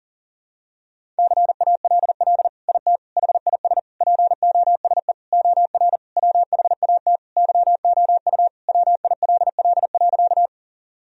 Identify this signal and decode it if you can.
Morse code, heard as a keyed tone, about 40 wpm, 700 Hz. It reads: CALL IT HIS POSE OR WHAT YOU WILL.
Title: Sentences from Top 1000 Words - Review Part 2 - 40wpm